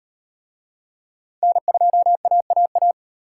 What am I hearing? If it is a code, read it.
Morse code, heard as a keyed tone, about 37 wpm, 700 Hz. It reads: N2AAA